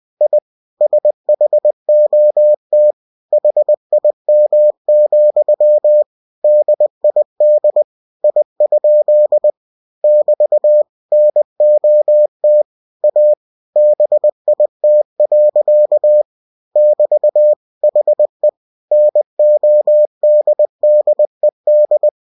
Morse code, heard as a keyed tone, about 20 wpm, 600 Hz. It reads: I SHOT HIM, DID I? <BT> NOT A BIT. <BT> HE NODDED